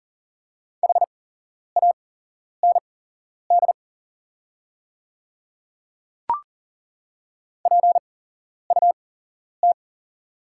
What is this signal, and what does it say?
Morse code, heard as a keyed tone, about 40 wpm, spaced as for 11 wpm, 700 Hz. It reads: HAND PUT
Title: 8x Character Spacing - Top 200 Words - 40wpm